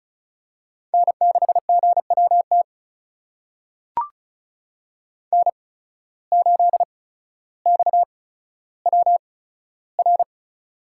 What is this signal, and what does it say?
Morse code, heard as a keyed tone, about 36 wpm, spaced as for 15 wpm, 700 Hz. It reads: N6GWT N8XWR
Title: USA Call Signs - 5 Characters - 8x Character Spacing - 35wpm